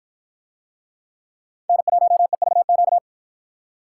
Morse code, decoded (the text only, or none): N1EVQ